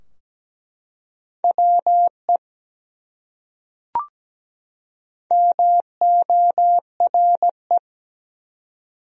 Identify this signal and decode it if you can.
Morse code, heard as a keyed tone, about 17 wpm, 700 Hz. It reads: WE MORE
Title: Top 100 Words 17wpm